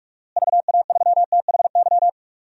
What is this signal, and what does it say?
Morse code, heard as a keyed tone, about 46 wpm, 700 Hz. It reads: VA3THY
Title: DXCC-100 Call Signs - 6 Characters - Repeated 3x - 45wpm